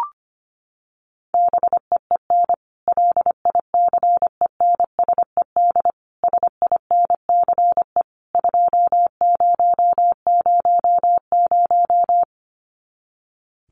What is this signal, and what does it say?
Morse code, heard as a keyed tone, about 25 wpm, 700 Hz. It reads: BEEN LICENSED SINCE 2000
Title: Common QSO Phrases 25wpm